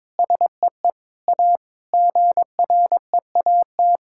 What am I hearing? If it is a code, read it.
Morse code, heard as a keyed tone, about 21 wpm, 700 Hz. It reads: SEE A GREAT
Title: Sentences from Top 1600 Words 22wpm